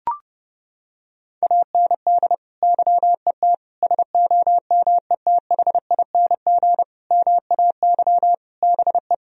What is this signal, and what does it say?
Morse code, heard as a keyed tone, about 30 wpm, 700 Hz. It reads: AND YET SOMETHING MAY BE
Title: Sentences from Top 1000 Words - Review Part 3 - 30wpm